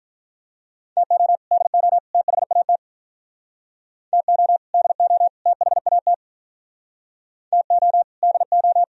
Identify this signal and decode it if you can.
Morse code, heard as a keyed tone, about 46 wpm, 700 Hz. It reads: TO DO THAT TO DO THAT TO DO